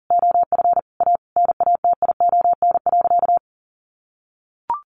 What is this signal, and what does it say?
Morse code, heard as a keyed tone, about 40 wpm, 700 Hz. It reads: OF A NATION.
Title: Sentences from Top 1400 Words 40wpm